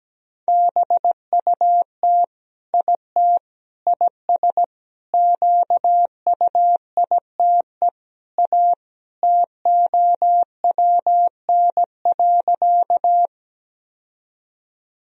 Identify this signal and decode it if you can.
Morse code, heard as a keyed tone, about 17 wpm, 700 Hz. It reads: BUT IT IS QUITE A TOWN.
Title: Sentences from Top 1000 Words - Review Part 2 - 17wpm